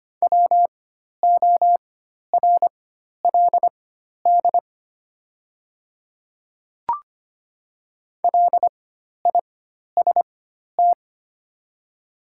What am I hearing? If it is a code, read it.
Morse code, heard as a keyed tone, about 25 wpm, spaced as for 12 wpm, 700 Hz. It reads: WORLD LIST
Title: 4x Character Spacing - Top 500 Words - 25wpm